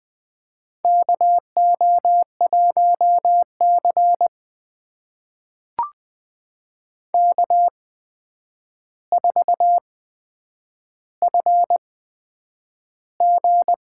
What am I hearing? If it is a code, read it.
Morse code, heard as a keyed tone, about 20 wpm, spaced as for 9 wpm, 700 Hz. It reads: KO1C K4FG
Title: USA Call Signs - 4 Characters - 8x Character Spacing - 20wpm